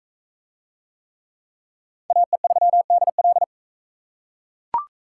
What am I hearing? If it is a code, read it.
Morse code, heard as a keyed tone, about 39 wpm, 700 Hz. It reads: AE3DL